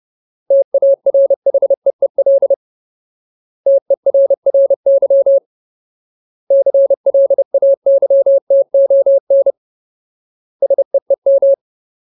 Morse code, heard as a keyed tone, about 30 wpm, 550 Hz. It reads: TARHEEL TERRY CLAYTON SEEM
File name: Ultimate Word List / 30WPM Ultimate